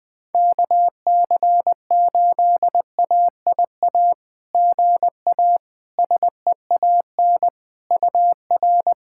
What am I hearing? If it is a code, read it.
Morse code, heard as a keyed tone, about 20 wpm, 700 Hz. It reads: KC8AIA GA SEAN UR
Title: SOTA QSO Practice 20wpm